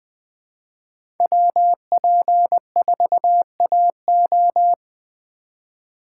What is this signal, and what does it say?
Morse code, heard as a keyed tone, about 20 wpm, 700 Hz. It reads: WP4AO
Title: USA Call Signs - 5 Characters - 2x Character Spacing - 20wpm